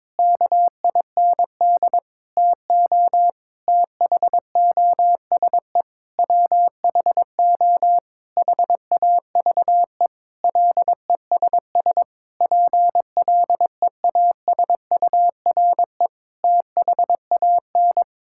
Morse code, heard as a keyed tone, about 22 wpm, 700 Hz. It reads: KIND TO THOSE WHO HAVE LESS PLEASURE THAN
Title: Sentences from Top 1600 Words 22wpm